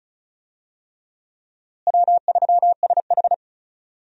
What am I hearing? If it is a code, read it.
Morse code, heard as a keyed tone, about 36 wpm, 700 Hz. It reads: W3SH
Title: USA Call Signs - 4 Characters - Repeated 3x - 35wpm